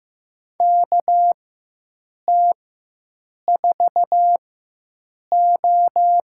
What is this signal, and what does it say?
Morse code, heard as a keyed tone, about 15 wpm, spaced as for 7 wpm, 700 Hz. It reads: KT4O
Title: USA Call Signs - 4x Character Spacing - 15wpm